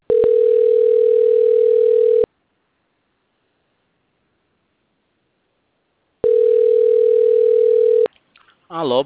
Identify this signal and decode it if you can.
Ringback tone (North America), 440 + 480 Hz, with speech.
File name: out- - - - - .60568